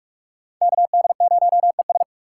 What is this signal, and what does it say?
Morse code, heard as a keyed tone, about 44 wpm, 700 Hz. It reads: KD0ES
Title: USA Call Signs - 5 Characters - 6x Character Spacing - 45wpm